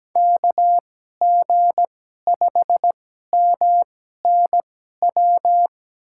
Morse code, heard as a keyed tone, about 17 wpm, spaced as for 12 wpm, 700 Hz. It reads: KG5MNW
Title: USA Call Signs - 6 Characters - 2x Character Spacing - 17wpm